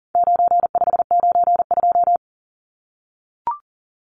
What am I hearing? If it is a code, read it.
Morse code, heard as a keyed tone, about 40 wpm, 700 Hz. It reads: 9592
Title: Numbers - 4-Digit 40wpm